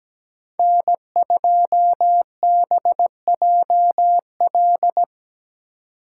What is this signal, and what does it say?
Morse code, heard as a keyed tone, about 20 wpm, 700 Hz. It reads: N2BJL
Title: Speed Racing - USA Call Signs 17wpm